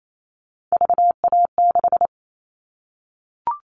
Morse code, heard as a keyed tone, about 28 wpm, 700 Hz. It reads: VA6
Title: Callsign Prefixes for Top 100 DXCC - 28wpm